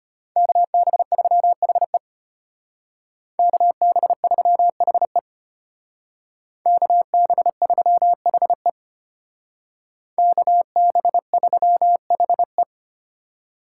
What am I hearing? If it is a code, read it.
Morse code, heard as a keyed tone, about 33 wpm, 700 Hz. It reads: KB3HE KB3HE KB3HE KB3HE